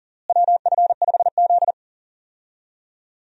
Morse code, heard as a keyed tone, about 40 wpm, 700 Hz. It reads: WF5Z
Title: USA Call Signs - 4 Characters - Repeated 3x - 40wpm